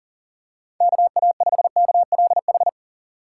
Morse code, heard as a keyed tone, about 37 wpm, 700 Hz. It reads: KA5KLH
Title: USA Call Signs - 6 Characters - 4x Character Spacing - 40wpm